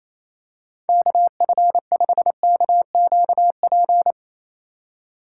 Morse code, heard as a keyed tone, about 27 wpm, 700 Hz. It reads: KF5KQP
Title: USA Call Signs - 6 Characters - 3x Character Spacing - 28wpm